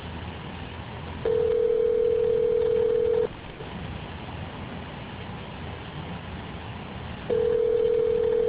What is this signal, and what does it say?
Ringback tone (North America), 440 + 480 Hz.